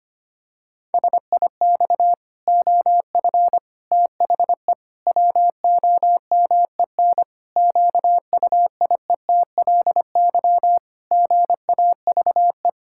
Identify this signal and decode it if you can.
Morse code, heard as a keyed tone, about 25 wpm, 700 Hz. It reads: SIX OF THE WOMEN QUIETLY GAVE
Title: Pangram Challenge v2 - 25wpm